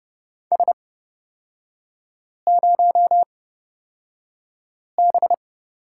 Morse code, heard as a keyed tone, about 30 wpm, spaced as for 5 wpm, 700 Hz. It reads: S0B
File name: Lesson 37e - Letter x - Three Characters 30wpm